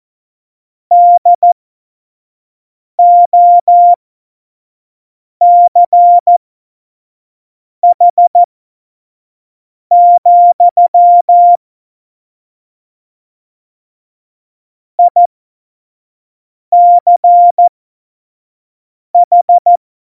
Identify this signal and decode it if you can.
Morse code, heard as a keyed tone, about 14 wpm, spaced as for 5 wpm, 700 Hz. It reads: DOCH, ICH